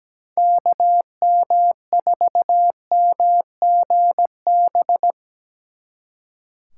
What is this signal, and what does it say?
Morse code, heard as a keyed tone, about 18 wpm, 700 Hz. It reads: KM4MGB
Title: Speed Racing - USA 2x3 Call Signs 17wpm